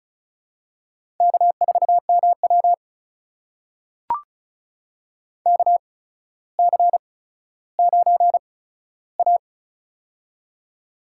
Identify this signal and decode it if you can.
Morse code, heard as a keyed tone, about 36 wpm, spaced as for 15 wpm, 700 Hz. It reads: K4MW KC9A
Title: USA Call Signs - 4 Characters - 8x Character Spacing - 35wpm